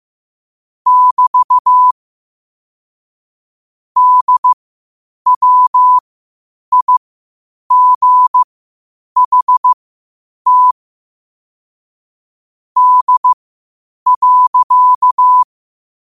Morse code, heard as a keyed tone, about 15 wpm, spaced as for 8 wpm, 1000 Hz. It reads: <BT> DWIGHT D.
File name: qotdfiles / 8wpm